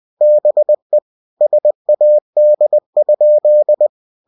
Morse code, heard as a keyed tone, about 20 wpm, 600 Hz. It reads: BE SAD?
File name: cw-books / winnie-the-pooh / winnie-20wpm-page-0046